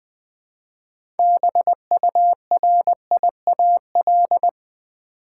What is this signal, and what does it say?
Morse code, heard as a keyed tone, about 21 wpm, 700 Hz. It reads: BURIAL
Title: 6 Letter Words - First 3 Letters Given - 20wpm